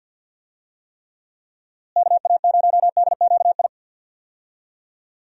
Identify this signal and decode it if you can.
Morse code, heard as a keyed tone, about 52 wpm, 700 Hz. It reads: KA0DQI
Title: USA Call Signs - 6 Characters - Repeated 2x - 50wpm